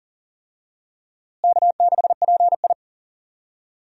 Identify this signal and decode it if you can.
Morse code, heard as a keyed tone, about 40 wpm, 700 Hz. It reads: K6PI
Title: USA Call Signs - 4 Characters - Repeated 2x - 40wpm